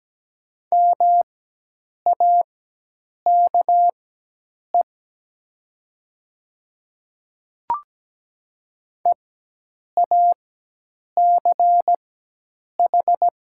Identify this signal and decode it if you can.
Morse code, heard as a keyed tone, about 17 wpm, spaced as for 8 wpm, 690 Hz. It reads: MAKE EACH
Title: 4x Character Spacing - Top 100 Words - 17wpm